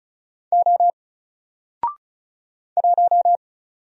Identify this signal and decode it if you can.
Morse code, heard as a keyed tone, about 36 wpm, 700 Hz. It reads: O 1
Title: Single Character (Letter:Number:Punctuation:Prosign) Alt 35wpm